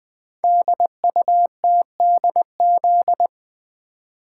Morse code, heard as a keyed tone, about 20 wpm, 700 Hz. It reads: DUTDZ
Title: Code Groups - 5 Letters - 20wpm